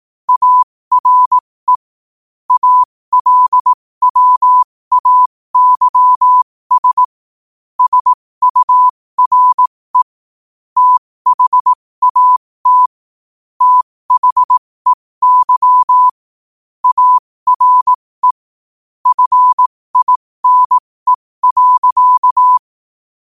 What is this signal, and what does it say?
Morse code, heard as a keyed tone, about 18 wpm, spaced as for 15 wpm, 1000 Hz. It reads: ARE ALWAYS SURE THAT THEY ARE FINE.
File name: qotdfiles / 15wpm